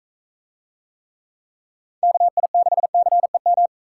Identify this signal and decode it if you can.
Morse code, heard as a keyed tone, about 41 wpm, 700 Hz. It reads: KI6CEM